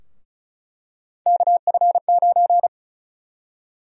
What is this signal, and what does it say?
Morse code, heard as a keyed tone, about 34 wpm, 700 Hz. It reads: KF9
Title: USA Call Sign Prefixes - 35wpm